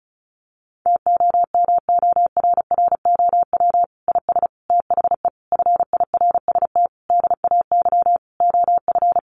Morse code, heard as a keyed tone, about 34 wpm, 700 Hz. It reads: TOMORROW IS THE FIRST DAY OF
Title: Additional Easy Phrases 35wpm